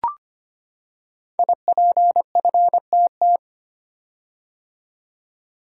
Morse code, heard as a keyed tone, about 25 wpm, 700 Hz. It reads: IPFTT
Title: Code Groups - 5 Letters - 25wpm